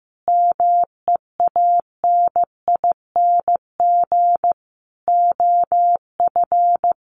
Morse code, heard as a keyed tone, about 15 wpm, 700 Hz. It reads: MEANING OF
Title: Sentences from Top 900 Words 15wpm